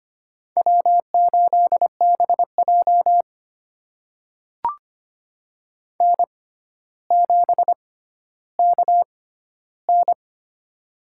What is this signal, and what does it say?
Morse code, heard as a keyed tone, about 25 wpm, spaced as for 13 wpm, 700 Hz. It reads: W8BJ N7KN
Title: USA Call Signs - 4 Characters - 6x Character Spacing - 25wpm